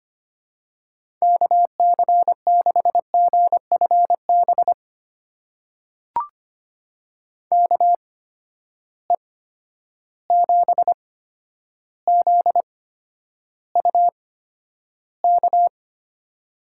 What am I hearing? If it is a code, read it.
Morse code, heard as a keyed tone, about 25 wpm, spaced as for 11 wpm, 700 Hz. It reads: KC6GFB KE7ZUK
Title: USA Call Signs - 6 Characters - 8x Character Spacing - 25wpm